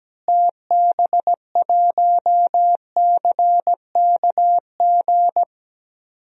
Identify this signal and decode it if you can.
Morse code, heard as a keyed tone, about 17 wpm, 700 Hz. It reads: TB1CKG
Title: DXCC-100 Call Signs - 6 Characters - Repeated 3x - 17wpm